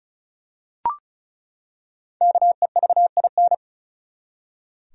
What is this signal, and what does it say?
Morse code, heard as a keyed tone, about 36 wpm, 700 Hz. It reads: KEVIN
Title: USA Common Names - Max 5 Letters - 35wpm